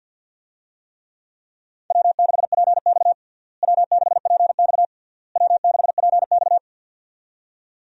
Morse code, heard as a keyed tone, about 49 wpm, 700 Hz. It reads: W6PX W6PX W6PX
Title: USA Call Signs - 4 Characters - Repeated 3x - 50wpm